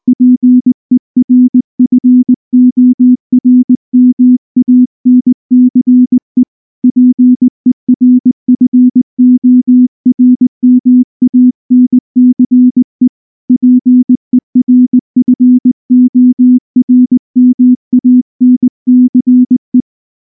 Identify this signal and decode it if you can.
Morse code, heard as a keyed tone, about 19 wpm, 250 Hz. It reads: PERFORMANCE PERFORMANCE PERFORMANCE
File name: morse / morse-training-20WPM-4